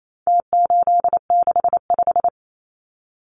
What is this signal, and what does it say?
Morse code, heard as a keyed tone, about 28 wpm, 700 Hz. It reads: T865
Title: Numbers - Contest Style - 4-Digit 28wpm